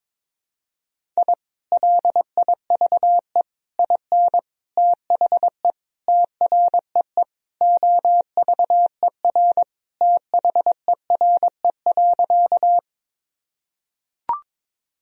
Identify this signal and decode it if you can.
Morse code, heard as a keyed tone, about 22 wpm, 700 Hz. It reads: I LIVE IN THE TREE OVER THERE.
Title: Sentences from Top 1000 Words - Review Part 1 - 22wpm